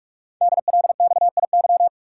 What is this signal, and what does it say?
Morse code, heard as a keyed tone, about 44 wpm, 700 Hz. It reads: DLXIY
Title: Code Groups - 5 Letters - 45wpm